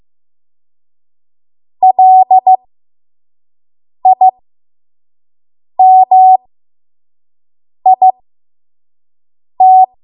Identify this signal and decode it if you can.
Morse code, heard as a keyed tone, about 15 wpm, spaced as for 5 wpm, 750 Hz. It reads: LIMIT
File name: Morse / 05WPM